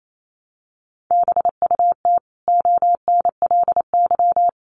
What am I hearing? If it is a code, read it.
Morse code, heard as a keyed tone, about 28 wpm, 700 Hz. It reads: BUT ONLY
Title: Sentences from Top 1100 Words 28wpm